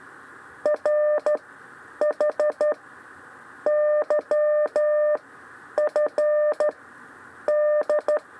Morse code, heard as a keyed tone, about 13 wpm, spaced as for 9 wpm, 600 Hz. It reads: RH YF D
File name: 6 lezione cw 2